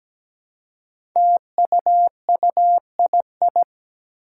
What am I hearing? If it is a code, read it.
Morse code, heard as a keyed tone, about 17 wpm, 700 Hz. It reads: TUUII
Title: Five Challenging Letter-Number Combinations 17wpm